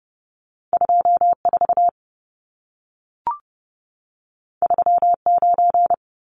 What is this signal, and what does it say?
Morse code, heard as a keyed tone, about 30 wpm, 700 Hz. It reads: 24 39
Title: Numbers - 2-Digit 30wpm